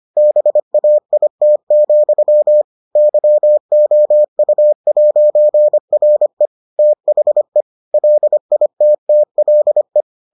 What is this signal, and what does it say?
Morse code, heard as a keyed tone, about 25 wpm, 600 Hz. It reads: BAIT, YOU'RE THE LITTLE